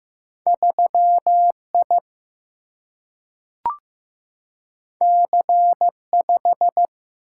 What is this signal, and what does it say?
Morse code, heard as a keyed tone, about 15 wpm, 700 Hz. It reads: 3I C5
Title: Single-Letter Single-Number 15wpm